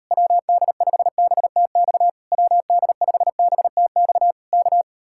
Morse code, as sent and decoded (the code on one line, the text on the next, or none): .-- -.. ..... -... - -..- / .-- -.. ..... -... - -..- / -.-
WD5BTX WD5BTX K